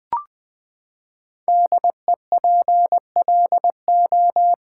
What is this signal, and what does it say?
Morse code, heard as a keyed tone, about 20 wpm, 700 Hz. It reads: DEPLO
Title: Build A Word - 5 Letters At A Time - 10-Letter Words - 20wpm